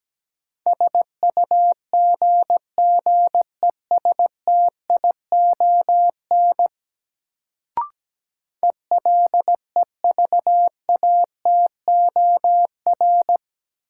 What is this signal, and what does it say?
Morse code, heard as a keyed tone, about 17 wpm, 700 Hz. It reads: SUGGESTION ELEVATOR
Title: Sentences from Top 1600 Words - New Word Drill Alt - 17wpm